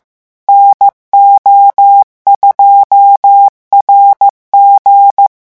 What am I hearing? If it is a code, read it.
Morse code, heard as a keyed tone, about 15 wpm, 800 Hz. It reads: NO2RG
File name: morse mp3 / revision13-vitesse 14 mots